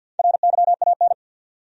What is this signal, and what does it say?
Morse code, heard as a keyed tone, about 52 wpm, 700 Hz. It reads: RYAN
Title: USA Common Names - 50wpm